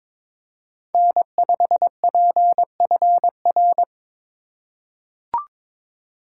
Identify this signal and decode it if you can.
Morse code, heard as a keyed tone, about 22 wpm, 700 Hz. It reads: N5PFR
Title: USA Call Signs - 5 Characters - 2x Character Spacing - 22wpm